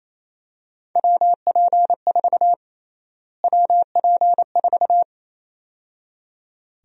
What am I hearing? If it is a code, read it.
Morse code, heard as a keyed tone, about 28 wpm, 700 Hz. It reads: WP4 WP4
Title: USA Call Sign Prefixes - Repeated 2x - 28wpm